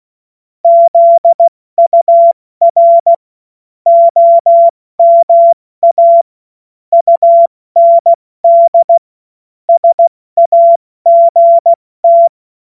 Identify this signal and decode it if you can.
Morse code, heard as a keyed tone, about 16 wpm, spaced as for 14 wpm, 680 Hz. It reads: ZUR OMA UND SAGT
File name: Morsekurs / Witze / Witz 28. September 2021, 70 F80